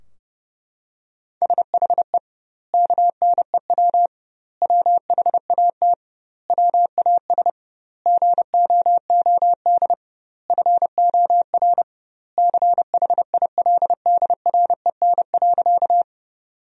Morse code, heard as a keyed tone, about 30 wpm, 700 Hz. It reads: SHE KNEW WHAT WAS GOOD FOR CHILDREN.
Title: 2x Word-Spacing - Sentences from Top 400 Words - 30wpm